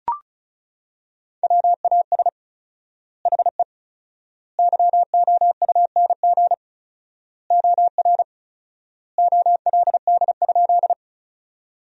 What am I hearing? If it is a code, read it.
Morse code, heard as a keyed tone, about 36 wpm, 700 Hz. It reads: WAS HE YOUNG OR OLD?
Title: 4x Word-Spacing - Sentences from Top 400 Words - 35wpm